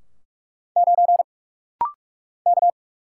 Morse code, heard as a keyed tone, about 46 wpm, 700 Hz. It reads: 9 K
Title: Single Letter-Number - Rapid-Fire 45wpm